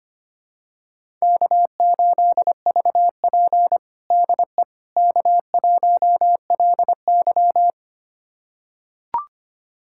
Morse code, heard as a keyed tone, about 25 wpm, 700 Hz. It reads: K8VP DE K1LY
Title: USA Callsign DE Callsign - 4 Characters - 25wpm